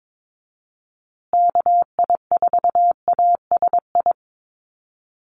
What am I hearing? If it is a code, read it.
Morse code, heard as a keyed tone, about 22 wpm, 700 Hz. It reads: KI4ASI
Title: USA Call Signs Repeated 3x 22wpm